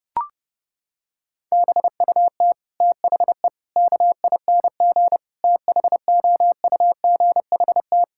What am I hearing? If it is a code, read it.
Morse code, heard as a keyed tone, about 32 wpm, 700 Hz. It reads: BUT THE KING THOUGHT
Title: Sentences from Top 1600 Words 30wpm